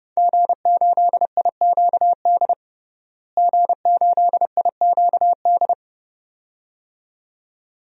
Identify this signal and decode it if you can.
Morse code, heard as a keyed tone, about 32 wpm, 700 Hz. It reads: G8IQD G8IQD
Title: DXCC-100 Call Signs - 5 Characters - Repeated 2x - 30wpm